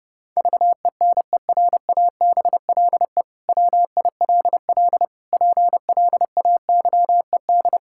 Morse code, heard as a keyed tone, about 28 wpm, 700 Hz. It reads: VENERABLE WILL PLAYED
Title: Pangram Challenge v2 - 30wpm